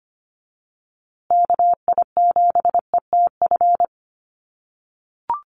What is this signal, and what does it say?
Morse code, heard as a keyed tone, about 25 wpm, 700 Hz. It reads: KI7ETF
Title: USA Call Signs Repeated 3x 25wpm